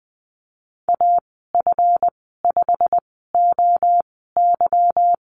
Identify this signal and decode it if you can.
Morse code, heard as a keyed tone, about 20 wpm, spaced as for 15 wpm, 700 Hz. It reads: AF5OY
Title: 207 Lesson 33g - Letter j - Call Signs 20wpm